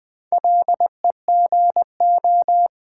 Morse code, heard as a keyed tone, about 20 wpm, 690 Hz. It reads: LEGO
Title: Difficult Words with Hints - 20wpm